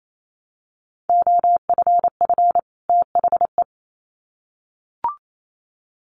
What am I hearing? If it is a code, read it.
Morse code, heard as a keyed tone, about 28 wpm, 700 Hz. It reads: OFF THE
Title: Sets of 2 Words 28wpm